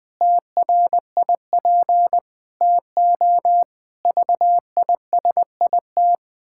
Morse code, heard as a keyed tone, about 20 wpm, 700 Hz. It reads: TRIP TO VISIT
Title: Pangram Challenge v2 - 20wpm